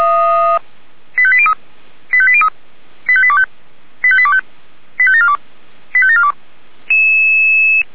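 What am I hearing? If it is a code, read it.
Five-tone selective call (ZVEI), heard as quick runs of five tones, tones of about 70 ms. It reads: 85803 85803 85825 85825 85742 85742
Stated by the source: ZVEI five-tone call received on an SDR.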